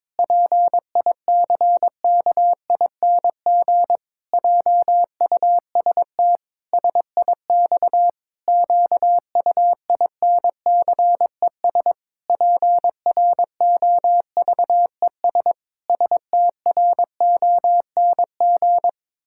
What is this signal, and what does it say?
Morse code, heard as a keyed tone, about 22 wpm, 700 Hz. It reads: PICKING JUST SIX QUINCES PROVES STRONG
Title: Pangram Challenge v2 - 22wpm